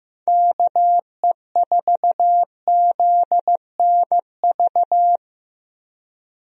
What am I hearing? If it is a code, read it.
Morse code, heard as a keyed tone, about 15 wpm, 700 Hz. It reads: KE4ZNV
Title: USA Call Signs - 6 Characters - Repeated 3x - 15wpm